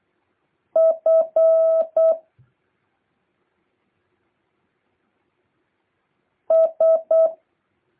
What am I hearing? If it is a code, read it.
Morse code, heard as a keyed tone, about 8 wpm, 650 Hz. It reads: F S
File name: jh8cbh / 2023cw